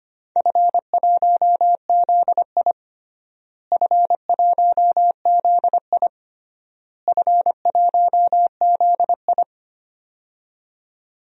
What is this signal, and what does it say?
Morse code, heard as a keyed tone, about 25 wpm, 700 Hz. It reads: F1ZI F1ZI F1ZI
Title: DXCC-100 Call Signs - 4 Characters - Repeated 3x - 25wpm